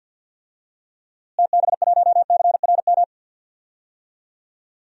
Morse code, heard as a keyed tone, about 51 wpm, 700 Hz. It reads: TB1CRM
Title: DXCC-100 Call Signs - 6 Characters - Repeated 3x - 50wpm